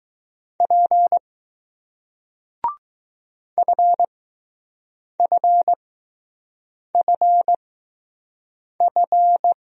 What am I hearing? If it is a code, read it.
Morse code, heard as a keyed tone, about 20 wpm, spaced as for 6 wpm, 700 Hz. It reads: P FFFF